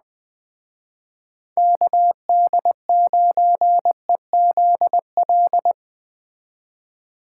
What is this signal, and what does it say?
Morse code, heard as a keyed tone, about 20 wpm, 700 Hz. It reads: KD9EZL
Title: Speed Racing - USA 2x3 Call Signs 17wpm